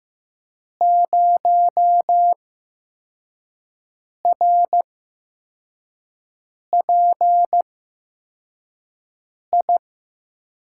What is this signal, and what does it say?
Morse code, heard as a keyed tone, about 15 wpm, spaced as for 4 wpm, 700 Hz. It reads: 0RPI
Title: USA Call Signs - 8x Character Spacing - 15wpm